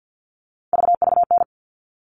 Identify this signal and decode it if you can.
Morse code, heard as a keyed tone, about 50 wpm, 700 Hz. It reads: VVN
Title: All Cut Numbers - 3-Digit 50wpm